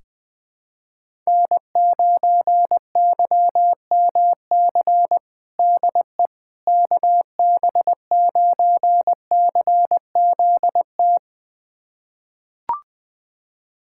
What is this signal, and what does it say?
Morse code, heard as a keyed tone, about 20 wpm, 700 Hz. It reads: N9YMC DE KB9CZT
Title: USA Callsign DE Callsign 20wpm